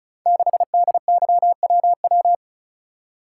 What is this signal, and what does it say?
Morse code, heard as a keyed tone, about 36 wpm, 700 Hz. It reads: 6DYWW
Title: Code Groups - 5 Letter:Numbers - 35wpm